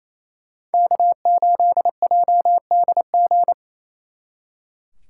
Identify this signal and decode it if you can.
Morse code, heard as a keyed tone, about 35 wpm, 700 Hz. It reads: K8JDG